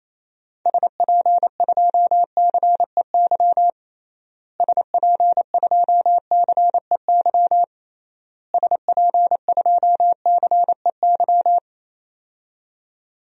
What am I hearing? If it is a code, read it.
Morse code, heard as a keyed tone, about 28 wpm, 700 Hz. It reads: SP2CEY SP2CEY SP2CEY
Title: DXCC-100 Call Signs - 6 Characters - Repeated 3x - 28wpm